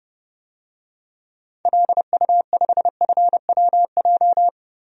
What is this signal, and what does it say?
Morse code, heard as a keyed tone, about 30 wpm, 700 Hz. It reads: LU5FWJ
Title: DXCC-100 Call Signs - 6 Characters - 30wpm